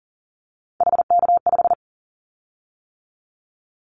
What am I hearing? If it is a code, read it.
Morse code, heard as a keyed tone, about 40 wpm, 690 Hz. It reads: HK5
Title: Callsign Prefixes for Top 100 DXCC - 40wpm